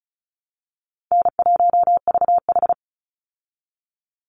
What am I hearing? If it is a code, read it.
Morse code, heard as a keyed tone, about 34 wpm, 700 Hz. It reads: N1VH
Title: Call Signs Repeated 3x - 35wpm